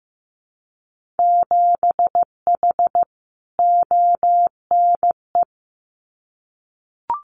Morse code, heard as a keyed tone, about 15 wpm, 700 Hz. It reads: 7H ONE
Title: Winter Field Day Exchanges Encore 15wpm